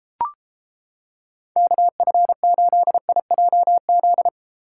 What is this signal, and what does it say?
Morse code, heard as a keyed tone, about 29 wpm, 700 Hz. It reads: KF8IJZ